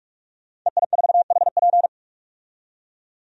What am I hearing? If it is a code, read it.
Morse code, heard as a keyed tone, about 46 wpm, 700 Hz. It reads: EI4HP
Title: DXCC-100 Call Signs - 5 Characters - 45wpm